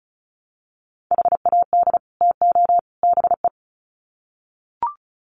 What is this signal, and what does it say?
Morse code, heard as a keyed tone, about 34 wpm, 700 Hz. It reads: HAD TO BE
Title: Sets of 3 Words 35wpm